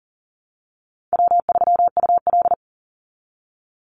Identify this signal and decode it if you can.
Morse code, heard as a keyed tone, about 40 wpm, 700 Hz. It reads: W3UL
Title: Easy Call Signs Repeated 2x - 40wpm